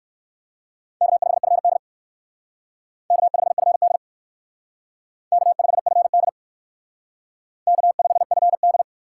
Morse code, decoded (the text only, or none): K5FD K5FD K5FD K5FD